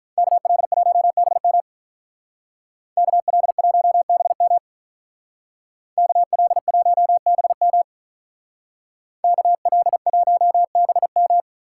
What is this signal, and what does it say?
Morse code, heard as a keyed tone, about 44 wpm, 700 Hz. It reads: KL1BM KL1BM KL1BM KL1BM
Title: Speed Racing - USA 2x2 Call Signs 35wpm